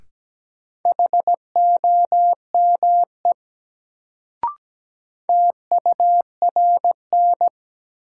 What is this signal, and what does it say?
Morse code, heard as a keyed tone, about 17 wpm, 700 Hz. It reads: HOME TURN
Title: Sentences from Top 200 Words - New Word Drill - 17wpm